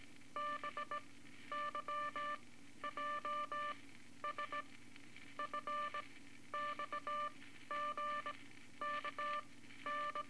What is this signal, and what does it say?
Morse code, heard as a keyed tone, about 19 wpm, spaced as for 12 wpm, 1180 Hz, with noise. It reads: BYJS FXGKN
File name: Morse Code Records / 18 A LCM 164 Ten Words Per Minute